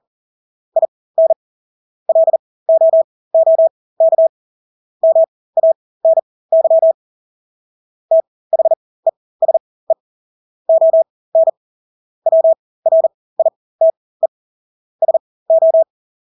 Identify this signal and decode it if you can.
Morse code, heard as a keyed tone, about 40 wpm, spaced as for 20 wpm, 650 Hz. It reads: IN LOOK MANY THESE ON WRITE SO
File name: Engelske ord 40-20 wpm